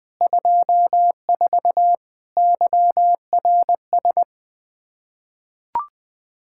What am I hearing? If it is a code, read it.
Morse code, heard as a keyed tone, about 20 wpm, 700 Hz. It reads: 24 YRS
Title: Practice QSO - Exchange 3 - More Info - 20wpm